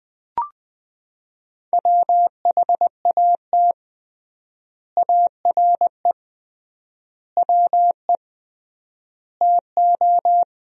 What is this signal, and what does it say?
Morse code, heard as a keyed tone, about 20 wpm, 700 Hz. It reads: WHAT ARE WE TO
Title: 3x Word-Spacing - Sentences from Top 100 Words - 20wpm